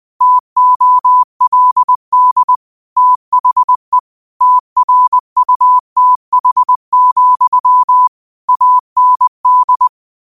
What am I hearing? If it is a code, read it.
Morse code, heard as a keyed tone, about 20 wpm, 1000 Hz. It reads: TOLD THE TRUTH, AND